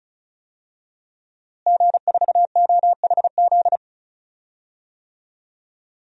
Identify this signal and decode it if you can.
Morse code, heard as a keyed tone, about 36 wpm, 700 Hz. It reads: G4OHZ
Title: DXCC-100 Call Signs - 5 Characters - Repeated 3x - 35wpm